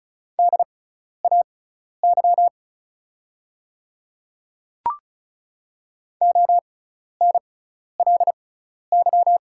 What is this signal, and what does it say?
Morse code, heard as a keyed tone, about 36 wpm, spaced as for 12 wpm, 700 Hz. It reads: DAY ONLY
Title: 6x Character Spacing - Top 300 Words - 35wpm